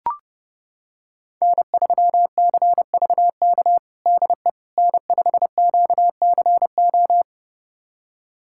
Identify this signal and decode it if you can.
Morse code, heard as a keyed tone, about 30 wpm, 700 Hz. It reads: N3CVK DE N5QCO
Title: USA Callsign DE Callsign - 5 Characters - 30wpm